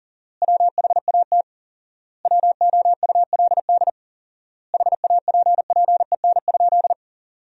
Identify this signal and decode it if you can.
Morse code, heard as a keyed tone, about 40 wpm, 700 Hz. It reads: WHAT WOULD HAPPEN?
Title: 4x Word-Spacing - Sentences from Top 400 Words - 40wpm